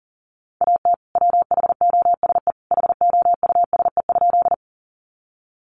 Morse code, heard as a keyed tone, about 40 wpm, 700 Hz. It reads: AT WHOSE HOUSE?
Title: Sentences from Top 1200 Words 40wpm